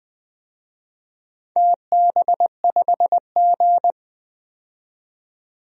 Morse code, heard as a keyed tone, about 20 wpm, 700 Hz. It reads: TB5G
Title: DXCC-100 Call Signs - 4 Characters - 20wpm